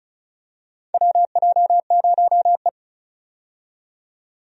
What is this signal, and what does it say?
Morse code, heard as a keyed tone, about 36 wpm, 700 Hz. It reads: WJ0E
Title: USA Call Signs - 4 Characters - 35wpm